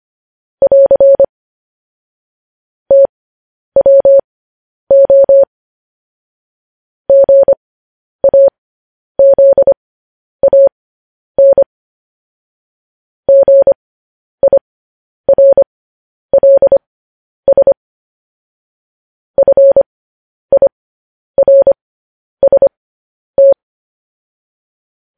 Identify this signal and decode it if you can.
Morse code, heard as a keyed tone, about 25 wpm, spaced as for 10 wpm, 550 Hz. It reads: <AR> TWO GAZAN GIRLS FIRST
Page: news-headlines-in-morse-code-at-10-wpm